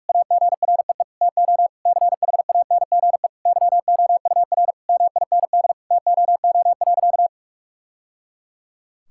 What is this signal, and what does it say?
Morse code, heard as a keyed tone, about 46 wpm, 700 Hz. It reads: AGREE TO CHANGE YOUR MIND TOO.
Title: Sentences from Top 1000 Words - Review Part 6 - 45wpm